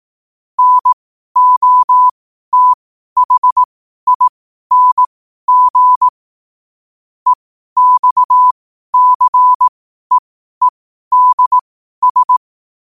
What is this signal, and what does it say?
Morse code, heard as a keyed tone, about 18 wpm, spaced as for 12 wpm, 1000 Hz. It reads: NOTHING EXCEEDS
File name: qotdfiles / 12wpm